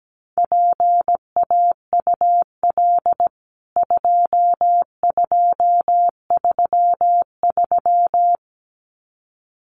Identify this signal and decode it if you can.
Morse code, heard as a keyed tone, about 17 wpm, 700 Hz. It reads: PAUL 2233
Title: CWT Exchange 17wpm